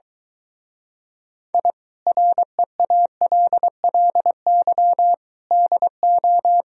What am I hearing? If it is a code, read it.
Morse code, heard as a keyed tone, about 22 wpm, 700 Hz. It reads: I REALLY DO